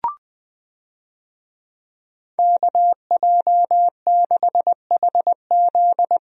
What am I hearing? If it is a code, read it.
Morse code, heard as a keyed tone, about 20 wpm, 700 Hz. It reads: KJ6HZ
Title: USA Call Signs - 5 Characters - Repeated 2x - 20wpm